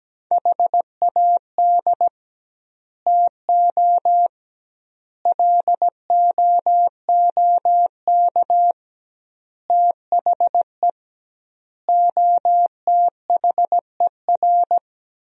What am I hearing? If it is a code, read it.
Morse code, heard as a keyed tone, about 17 wpm, 700 Hz. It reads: HAD TO LOOK THE OTHER
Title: 2x Word-Spacing - Sentences from Top 100 Words - 17wpm